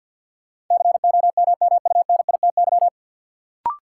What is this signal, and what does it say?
Morse code, heard as a keyed tone, about 51 wpm, 700 Hz. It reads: COMMUNITY
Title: Sentences from Top 1200 Words - New Word Drill Alt - 50wpm